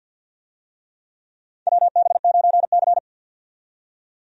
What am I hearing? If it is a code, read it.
Morse code, heard as a keyed tone, about 51 wpm, 700 Hz. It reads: WB9C
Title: USA Call Signs - 4 Characters - 50wpm